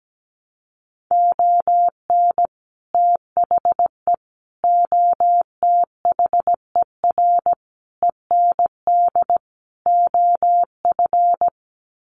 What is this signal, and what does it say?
Morse code, heard as a keyed tone, about 17 wpm, 700 Hz. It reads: ON THE OTHER END OF
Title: Sets of 5 Words Encore 17wpm